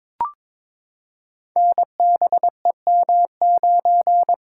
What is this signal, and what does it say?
Morse code, heard as a keyed tone, about 23 wpm, 700 Hz. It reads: NBEM9
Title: Code Groups - 5 Letter:Numbers - 22wpm